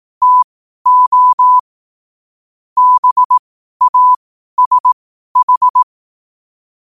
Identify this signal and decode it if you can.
Morse code, heard as a keyed tone, about 18 wpm, spaced as for 12 wpm, 1000 Hz. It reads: TO BASH
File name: qotdfiles / 12wpm